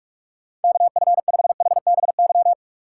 Morse code, heard as a keyed tone, about 56 wpm, 700 Hz. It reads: KF5HBY